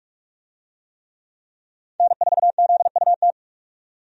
Morse code, heard as a keyed tone, about 46 wpm, 700 Hz. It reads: N4ZUT
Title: USA Call Signs - 5 Characters - 45wpm